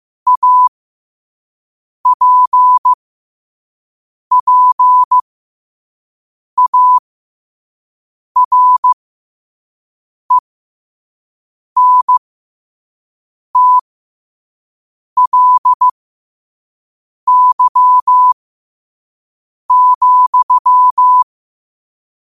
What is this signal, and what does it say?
Morse code, heard as a keyed tone, about 15 wpm, spaced as for 5 wpm, 1000 Hz. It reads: APPARENTLY,
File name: qotdfiles / 5wpm